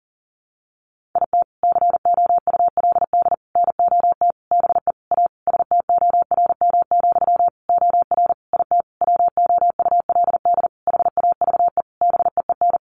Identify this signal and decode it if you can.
Morse code, heard as a keyed tone, about 40 wpm, 700 Hz. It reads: IT COULD NOT BE A STORM, OR IT WOULD HAVE BEEN
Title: Sentences from Top 1500 Words 40wpm